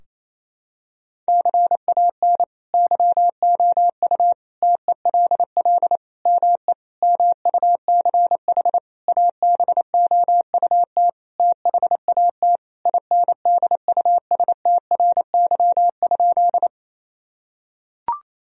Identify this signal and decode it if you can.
Morse code, heard as a keyed tone, about 28 wpm, 700 Hz. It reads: CAN YOU TELL ME MUCH ABOUT THAT INDUSTRY?
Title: Sentences from Top 1000 Words - Review Part 4 - 28wpm